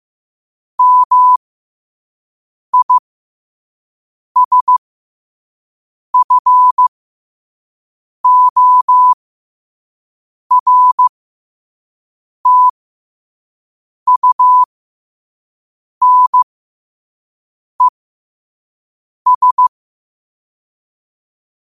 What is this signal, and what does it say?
Morse code, heard as a keyed tone, about 15 wpm, spaced as for 5 wpm, 1000 Hz. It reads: MISFORTUNES
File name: qotdfiles / 5wpm